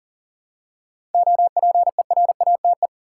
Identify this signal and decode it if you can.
Morse code, heard as a keyed tone, about 40 wpm, 700 Hz. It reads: OPERATE
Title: Seven Letter Words - 40wpm